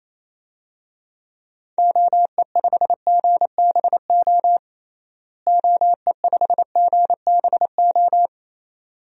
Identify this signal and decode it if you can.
Morse code, heard as a keyed tone, about 28 wpm, 700 Hz. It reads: OE5GBO OE5GBO
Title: DXCC-100 Call Signs - 6 Characters - Repeated 2x - 28wpm